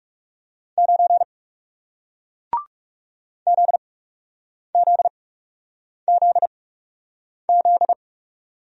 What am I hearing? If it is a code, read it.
Morse code, heard as a keyed tone, about 41 wpm, spaced as for 8 wpm, 700 Hz. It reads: 9 ZZZZ